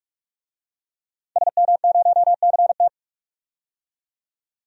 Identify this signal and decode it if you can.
Morse code, heard as a keyed tone, about 44 wpm, 700 Hz. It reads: SM0CT
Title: DXCC-100 Call Signs - 5 Characters - 45wpm